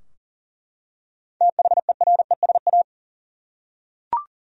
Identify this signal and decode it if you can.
Morse code, heard as a keyed tone, about 40 wpm, 700 Hz. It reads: THERESA
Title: USA Common Names - 40wpm